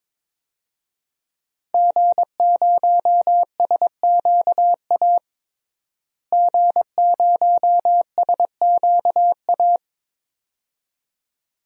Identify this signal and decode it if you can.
Morse code, heard as a keyed tone, about 23 wpm, 700 Hz. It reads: G0SQA G0SQA
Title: DXCC-100 Call Signs - 5 Characters - Repeated 2x - 22wpm